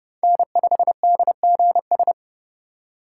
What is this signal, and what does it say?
Morse code, heard as a keyed tone, about 29 wpm, 700 Hz. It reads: N5DGS